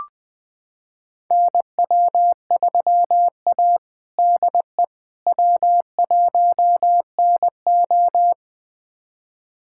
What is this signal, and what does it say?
Morse code, heard as a keyed tone, about 21 wpm, 700 Hz. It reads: NW3A DE W1NO
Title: USA Callsign DE Callsign - 4 Characters - 20wpm